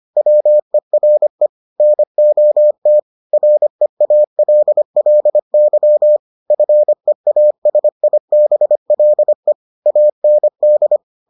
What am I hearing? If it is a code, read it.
Morse code, heard as a keyed tone, about 25 wpm, 600 Hz. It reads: WERE NOT REALLY FEASIBLE AND